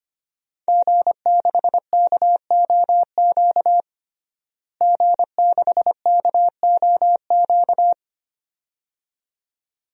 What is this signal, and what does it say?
Morse code, heard as a keyed tone, about 25 wpm, 700 Hz. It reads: G6KOQ G6KOQ
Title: DXCC-100 Call Signs - 5 Characters - Repeated 2x - 25wpm